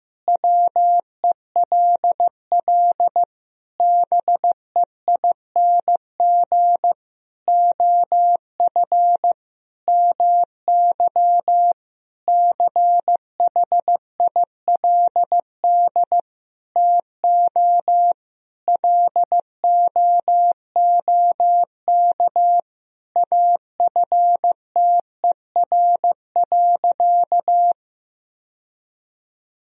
Morse code, heard as a keyed tone, about 15 wpm, 700 Hz. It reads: WELL BEING OF MY CHILD TO LOOK AFTER.
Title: Sentences from Top 1000 Words - Review Part 3 - 15wpm